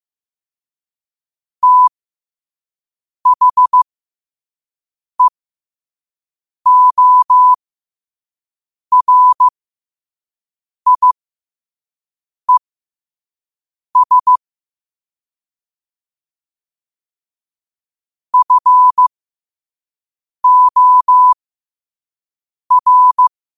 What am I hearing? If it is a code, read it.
Morse code, heard as a keyed tone, about 15 wpm, spaced as for 5 wpm, 1000 Hz. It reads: THEORIES FOR
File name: qotdfiles / 5wpm